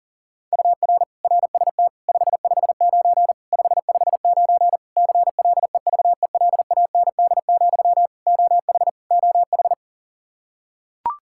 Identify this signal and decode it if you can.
Morse code, heard as a keyed tone, about 40 wpm, 700 Hz. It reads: UR RST 559 559 CLEVELAND, OH OH
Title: K1USN SST QSO Practice 40wpm